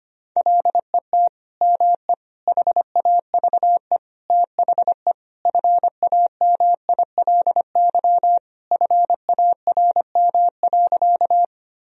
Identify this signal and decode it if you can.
Morse code, heard as a keyed tone, about 25 wpm, 700 Hz. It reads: LET ME HAVE THE FAMILY FARM.
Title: Sentences from Top 1000 Words - Review Part 2 - 25wpm